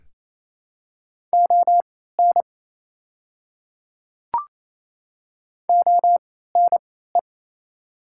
Morse code, heard as a keyed tone, about 27 wpm, spaced as for 16 wpm, 700 Hz. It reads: ON ONE
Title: 3x Character Spacing - Top 200 Words - 28wpm